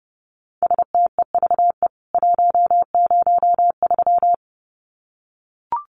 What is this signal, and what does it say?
Morse code, heard as a keyed tone, about 30 wpm, 700 Hz. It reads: STEVE 103
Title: CWT Exchange 30wpm